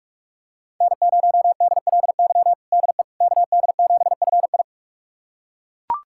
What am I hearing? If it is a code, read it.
Morse code, heard as a keyed tone, about 41 wpm, 700 Hz. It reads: N0DLY DE KD7FI
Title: USA Callsign DE Callsign - 5 Characters - 45wpm